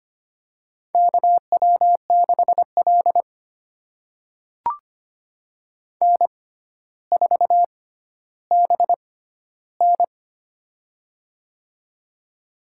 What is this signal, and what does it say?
Morse code, heard as a keyed tone, about 25 wpm, spaced as for 13 wpm, 700 Hz. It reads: KW6L N4BN
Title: USA Call Signs - 4 Characters - 6x Character Spacing - 25wpm